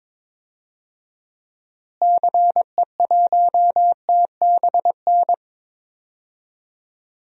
Morse code, heard as a keyed tone, about 22 wpm, 700 Hz. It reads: CE1TBN
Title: DXCC-100 Call Signs - 6 Characters - Repeated 3x - 22wpm